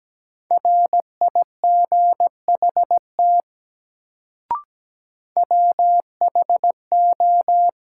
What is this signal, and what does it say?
Morse code, heard as a keyed tone, about 17 wpm, 700 Hz. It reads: RIGHT WHO
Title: Sets of 2 Words - New Word Drill - 17wpm